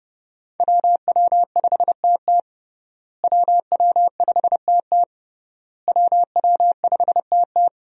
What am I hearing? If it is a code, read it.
Morse code, heard as a keyed tone, about 31 wpm, 700 Hz. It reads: WW5TT WW5TT WW5TT
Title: USA Call Signs - 5 Characters - Repeated 3x - 30wpm